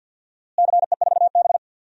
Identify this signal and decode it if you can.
Morse code, heard as a keyed tone, about 52 wpm, 700 Hz. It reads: CE4B
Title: DXCC-100 Call Signs - 4 Characters - Repeated 2x - 50wpm